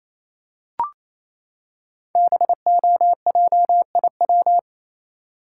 Morse code, heard as a keyed tone, about 28 wpm, 700 Hz. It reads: BOJIW
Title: Code Groups - 5 Letters - 28wpm